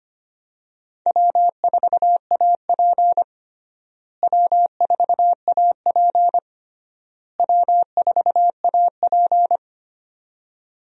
Morse code, heard as a keyed tone, about 25 wpm, 700 Hz. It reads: W4AP W4AP W4AP
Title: USA Call Signs - 4 Characters - Repeated 3x - 25wpm